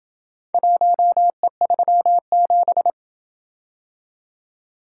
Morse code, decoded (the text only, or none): 1E37